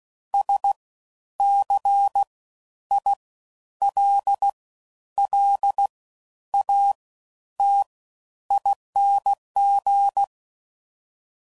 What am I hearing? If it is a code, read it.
Morse code, heard as a keyed tone, about 16 wpm, 800 Hz. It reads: S C I L L A T ING
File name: Suffixes / Special INGs / Fast Ings